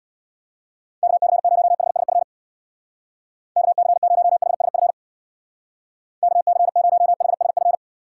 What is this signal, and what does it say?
Morse code, heard as a keyed tone, about 64 wpm, 700 Hz. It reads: KC0HSF KC0HSF KC0HSF